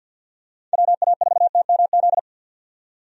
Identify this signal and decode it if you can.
Morse code, heard as a keyed tone, about 51 wpm, 700 Hz. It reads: WA4TMZ
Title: USA Call Signs - 6 Characters - 50wpm